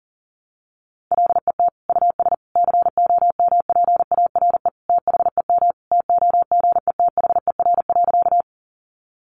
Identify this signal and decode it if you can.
Morse code, heard as a keyed tone, about 40 wpm, 700 Hz. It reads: LET US COMPARE THEM TOGETHER.
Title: Sentences from Top 1300 Words 40wpm